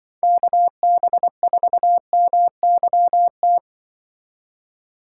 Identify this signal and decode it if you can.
Morse code, heard as a keyed tone, about 24 wpm, 700 Hz. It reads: KB4MYT
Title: Speed Racing - USA 2x3 Call Signs 20wpm